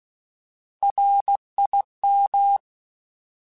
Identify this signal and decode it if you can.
Morse code, heard as a keyed tone, about 16 wpm, 800 Hz. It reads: RIM
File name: Alphabetical / Alphabetical mp3s / Rs @15